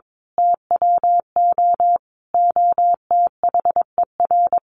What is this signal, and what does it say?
Morse code, heard as a keyed tone, about 22 wpm, 700 Hz. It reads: TWO OTHER
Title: A Sound Of Thunder 22wpm